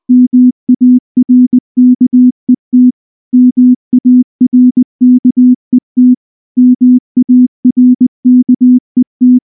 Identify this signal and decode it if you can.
Morse code, heard as a keyed tone, about 19 wpm, 250 Hz. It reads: MARKET MARKET MARKET
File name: morse / morse-training-20WPM-2